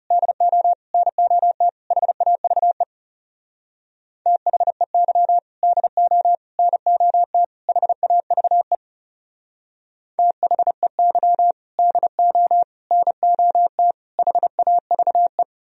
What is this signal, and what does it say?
Morse code, heard as a keyed tone, about 36 wpm, 700 Hz. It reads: DO NOT HAVE THEY DO NOT HAVE THEY DO NOT HAVE
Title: Speed-Racing - Sets of 4 Words 30wpm